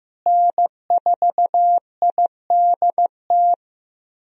Morse code, heard as a keyed tone, about 16 wpm, 700 Hz. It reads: N4IDT
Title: USA Call Signs - 5 Characters - Repeated 3x - 15wpm